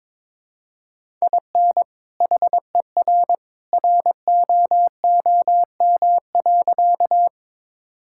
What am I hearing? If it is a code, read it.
Morse code, heard as a keyed tone, about 22 wpm, 700 Hz. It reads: IN HER ROOM.
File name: practice / 22wpm / Sentences with 2-Word Phrases - 22wpm